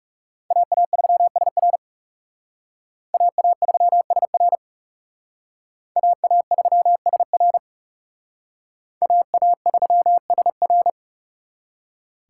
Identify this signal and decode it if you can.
Morse code, heard as a keyed tone, about 36 wpm, 700 Hz. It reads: AA3SR AA3SR AA3SR AA3SR
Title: Speed Racing - USA 2x2 Call Signs 30wpm